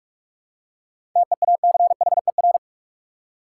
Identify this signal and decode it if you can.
Morse code, heard as a keyed tone, about 46 wpm, 700 Hz. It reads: TEACHER
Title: Seven Letter Words - 45wpm